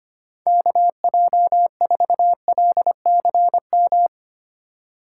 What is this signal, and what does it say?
Morse code, heard as a keyed tone, about 25 wpm, 700 Hz. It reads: KJ4LCM
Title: USA Call Signs - 6 Characters - 4x Character Spacing - 25wpm